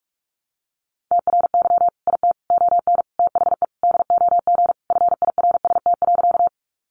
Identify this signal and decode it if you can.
Morse code, heard as a keyed tone, about 44 wpm, 700 Hz. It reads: TRY IT ON THE DOG FIRST.
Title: Sentences from Top 400 Words 45wpm